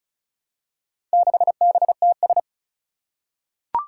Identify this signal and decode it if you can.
Morse code, heard as a keyed tone, about 36 wpm, 700 Hz. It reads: 6BTS
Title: Four Challenging Letter-Number Combinations 35wpm